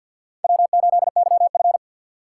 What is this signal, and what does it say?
Morse code, heard as a keyed tone, about 49 wpm, 690 Hz. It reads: W8YF
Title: USA Call Signs - 2x Character Spacing - 50wpm